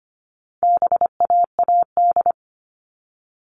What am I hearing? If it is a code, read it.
Morse code, heard as a keyed tone, about 25 wpm, 700 Hz. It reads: BAAD
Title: All Cut Numbers - 4-Digit 25wpm